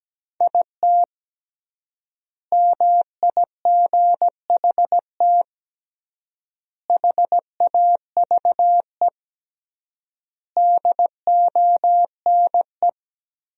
Morse code, heard as a keyed tone, about 17 wpm, 700 Hz. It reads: IT MIGHT HAVE DONE
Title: 3x Word-Spacing - Sentences from Top 500 Words - 17wpm